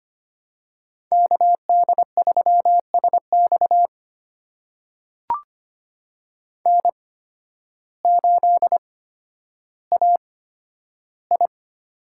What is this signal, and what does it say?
Morse code, heard as a keyed tone, about 25 wpm, spaced as for 12 wpm, 700 Hz. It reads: KD3SX N8AI
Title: USA Call Signs - 5 Characters - 8x Character Spacing - 25wpm